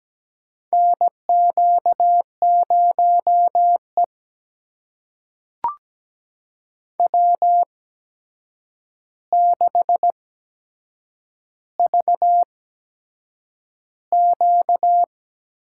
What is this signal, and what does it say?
Morse code, heard as a keyed tone, about 17 wpm, spaced as for 7 wpm, 700 Hz. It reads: NQ0E W6VQ
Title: USA Call Signs - 4 Characters - 8x Character Spacing - 17wpm